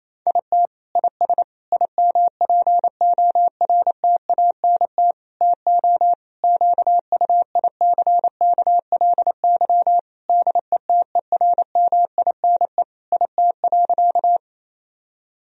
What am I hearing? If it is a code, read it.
Morse code, heard as a keyed tone, about 28 wpm, 700 Hz. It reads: IT IS IMPORTANT TO QUICKLY DETERMINE IT.
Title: Sentences from Top 1000 Words - Review Part 4 - 28wpm